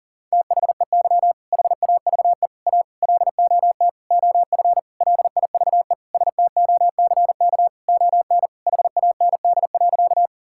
Morse code, heard as a keyed tone, about 40 wpm, 700 Hz. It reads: THEY HAVE A LOT OF LIVE STOCK ON HAND.
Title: Sentences from Top 1000 Words - Review Part 5 - 40wpm